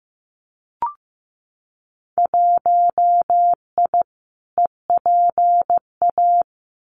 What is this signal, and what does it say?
Morse code, heard as a keyed tone, about 15 wpm, 700 Hz. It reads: 1I EPA
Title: Winter Field Day Exchanges Encore 15wpm